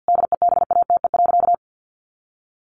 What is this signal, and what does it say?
Morse code, heard as a keyed tone, about 48 wpm, 700 Hz. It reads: DEBATE.
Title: Sentences from Top 1100 Words 50wpm